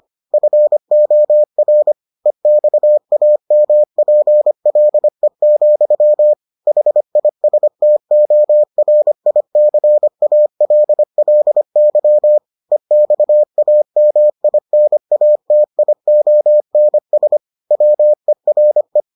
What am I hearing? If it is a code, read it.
Morse code, heard as a keyed tone, about 25 wpm, 600 Hz. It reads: FOR EXAMPLE, HISTORICALLY EXAMINATIONS WERE